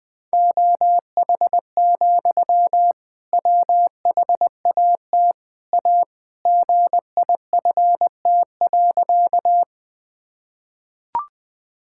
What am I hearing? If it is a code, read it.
Morse code, heard as a keyed tone, about 21 wpm, 700 Hz. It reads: OH, WHAT A GIFT.
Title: Sentences from Top 1000 Words - Review Part 6 - 20wpm